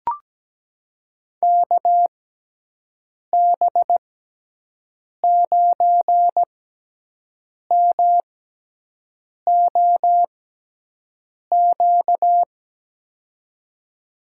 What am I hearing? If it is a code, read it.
Morse code, heard as a keyed tone, about 17 wpm, spaced as for 6 wpm, 700 Hz. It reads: KB9MOQ
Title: USA Call Signs - 6 Characters - 6x Character Spacing - 17wpm